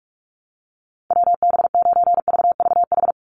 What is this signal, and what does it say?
Morse code, heard as a keyed tone, about 44 wpm, 700 Hz. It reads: WB9VVH
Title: USA Call Signs 45wpm